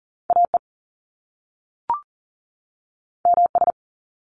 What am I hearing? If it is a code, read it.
Morse code, heard as a keyed tone, about 38 wpm, 700 Hz. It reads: AE MS
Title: Top Two Letter Combinations in Words 40wpm